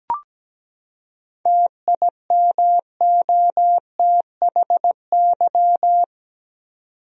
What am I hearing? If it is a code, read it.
Morse code, heard as a keyed tone, about 17 wpm, 700 Hz. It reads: TIMOTHY
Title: USA Common Names - 17wpm